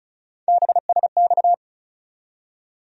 Morse code, heard as a keyed tone, about 34 wpm, 700 Hz. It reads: BSX
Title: USA Call Sign Postfixes - 3 Letters 35wpm